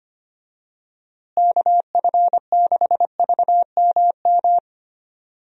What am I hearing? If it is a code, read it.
Morse code, heard as a keyed tone, about 25 wpm, 700 Hz. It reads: KF6VMM
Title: USA Call Signs - 6 Characters - Repeated 3x - 25wpm